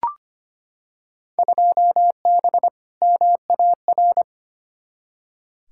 Morse code, heard as a keyed tone, about 25 wpm, 700 Hz. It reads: 2B MAR
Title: Field Day Exchanges Encore - 25wpm